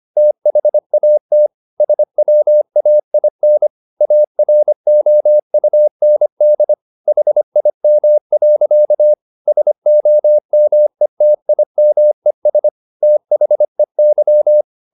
Morse code, heard as a keyed tone, about 25 wpm, 600 Hz. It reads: THAT SWAIN AROUND HIM. SOMETIMES THEY